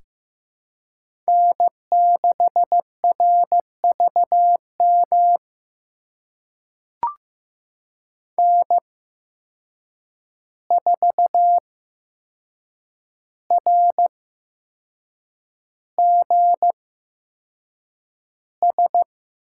Morse code, heard as a keyed tone, about 15 wpm, spaced as for 6 wpm, 700 Hz. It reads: N6RVM N4RGS
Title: USA Call Signs - 5 Characters - 8x Character Spacing - 15wpm